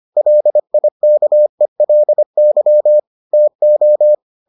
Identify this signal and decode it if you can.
Morse code, heard as a keyed tone, about 25 wpm, 600 Hz. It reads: LIKELY TO